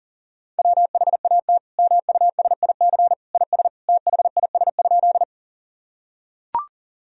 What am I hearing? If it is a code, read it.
Morse code, heard as a keyed tone, about 40 wpm, 700 Hz. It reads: WHAT MUSIC IS THIS?
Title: Sentences from Top 1000 Words - Review Part 1 - 40wpm